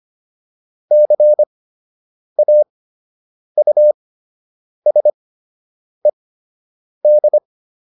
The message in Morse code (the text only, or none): CAUSED